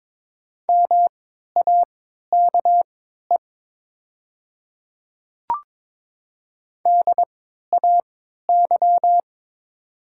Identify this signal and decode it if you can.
Morse code, heard as a keyed tone, about 22 wpm, spaced as for 13 wpm, 700 Hz. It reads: MAKE DAY
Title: 3x Character Spacing - Top 100 Words - 22wpm